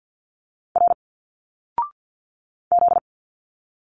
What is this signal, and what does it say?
Morse code, heard as a keyed tone, about 50 wpm, 700 Hz. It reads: R Z
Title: Single Letters 50wpm